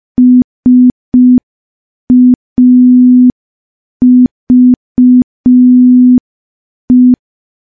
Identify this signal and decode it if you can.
Morse code, heard as a keyed tone, about 5 wpm, 250 Hz. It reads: SAVE
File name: morse / morse-training-05WPM